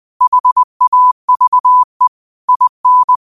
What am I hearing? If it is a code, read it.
Morse code, heard as a keyed tone, about 20 wpm, 1000 Hz. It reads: HAVE IN
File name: qotdfiles / 20wpm